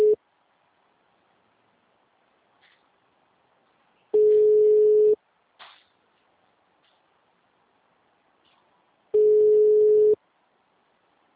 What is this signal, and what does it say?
Ringback tone (Europe, 425 Hz).